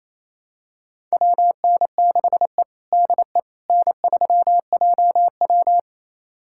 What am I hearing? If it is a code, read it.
Morse code, heard as a keyed tone, about 28 wpm, 700 Hz. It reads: WN6E DE N3JW
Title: USA Callsign DE Callsign - 4 Characters - 28wpm